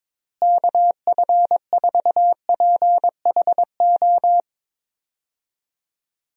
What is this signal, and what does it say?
Morse code, heard as a keyed tone, about 22 wpm, 700 Hz. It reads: KF4PHO
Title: USA Call Signs - 6 Characters - Repeated 3x - 22wpm